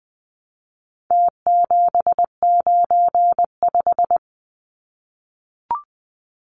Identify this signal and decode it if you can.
Morse code, heard as a keyed tone, about 20 wpm, 700 Hz. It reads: T795
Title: Numbers - Contest Style - 4-Digit 20wpm